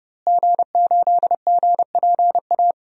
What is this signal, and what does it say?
Morse code, heard as a keyed tone, about 30 wpm, 700 Hz. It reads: G8GPA
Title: DXCC-100 Call Signs - 5 Characters - Repeated 2x - 30wpm